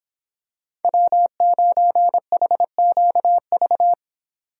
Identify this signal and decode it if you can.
Morse code, heard as a keyed tone, about 26 wpm, 700 Hz. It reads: W9HQV